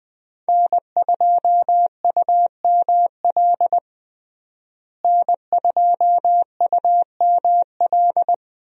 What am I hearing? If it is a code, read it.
Morse code, heard as a keyed tone, about 20 wpm, 700 Hz. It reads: N2UML N2UML
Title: USA Call Signs - 5 Characters - Repeated 3x - 20wpm